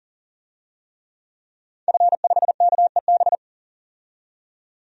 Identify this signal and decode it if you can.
Morse code, heard as a keyed tone, about 40 wpm, 700 Hz. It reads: F5KEB
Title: DXCC-100 Call Signs - 5 Characters - 40wpm